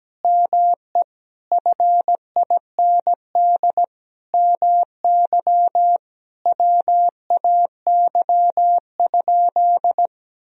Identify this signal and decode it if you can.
Morse code, heard as a keyed tone, about 17 wpm, 700 Hz. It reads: ME FIND MY WAY?
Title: Sentences from Top 200 Words and Abbreviations 17wpm